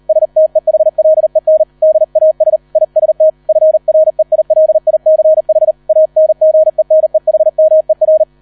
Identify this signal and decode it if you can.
Morse code, heard as a keyed tone, about 38 wpm, 630 Hz. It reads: STEHLEN DAS IST FREILICH ANGENEHMER